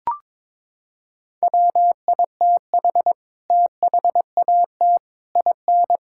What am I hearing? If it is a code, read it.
Morse code, heard as a keyed tone, about 22 wpm, 700 Hz. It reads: WITH THAT IN
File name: Sets of 9 Words - 22wpm